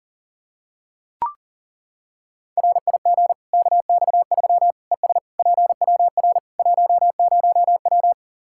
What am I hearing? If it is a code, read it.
Morse code, heard as a keyed tone, about 40 wpm, 700 Hz. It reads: RIG KX3 ES PWR 10W
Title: Practice QSO - Exchange 2 - Info - 40wpm